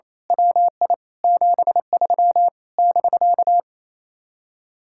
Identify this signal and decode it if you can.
Morse code, heard as a keyed tone, about 26 wpm, 700 Hz. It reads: WI 73 <BK>
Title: Speed-Racing Common QSO Phrases - 28wpm